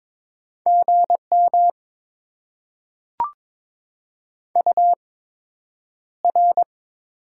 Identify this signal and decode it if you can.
Morse code, heard as a keyed tone, about 22 wpm, spaced as for 9 wpm, 700 Hz. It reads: GM UR
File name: Top 90 QSO Elements - 8x Character Spacing - 22wpm